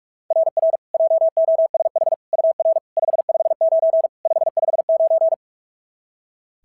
Morse code, heard as a keyed tone, about 44 wpm, 690 Hz. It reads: RR JOSH UR 559 559
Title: SOTA QSO Practice 45wpm